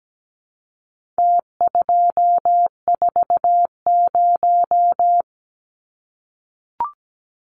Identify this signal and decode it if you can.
Morse code, heard as a keyed tone, about 17 wpm, 700 Hz. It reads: T240
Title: Numbers - Contest Style - 4-Digit 17wpm